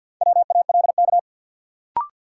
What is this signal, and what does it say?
Morse code, heard as a keyed tone, about 51 wpm, 700 Hz. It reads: WALK
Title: Sentences from Top 300 Words - New Word Drill - 50wpm